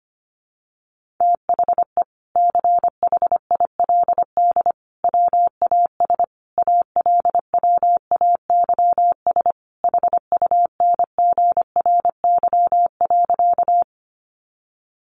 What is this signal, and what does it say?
Morse code, heard as a keyed tone, about 25 wpm, 700 Hz. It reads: THE CHILD WAS ALWAYS HUNGRY.
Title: Sentences from Top 1400 Words 25wpm